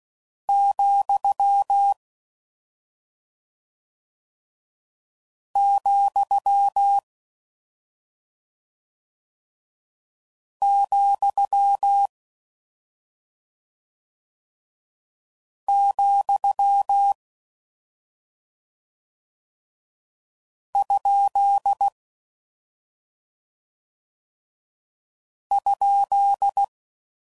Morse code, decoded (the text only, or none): , , , , ? ?